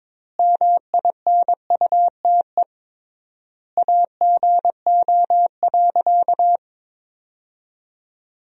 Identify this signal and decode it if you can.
Morse code, heard as a keyed tone, about 21 wpm, 700 Hz. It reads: MINUTE AGO.
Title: 3x Word-Spacing - Sentences from Top 500 Words - 22wpm